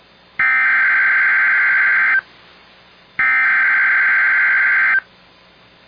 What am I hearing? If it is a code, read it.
Emergency Alert System (SAME) tones (header bursts), heard as harsh buzzing bursts. It is heard from 0.4 to 4.9 s.